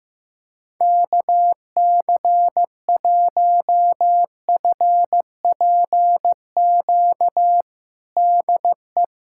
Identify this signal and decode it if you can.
Morse code, heard as a keyed tone, about 15 wpm, 700 Hz. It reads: KC1FPQ DE
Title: USA Callsign DE Callsign 15wpm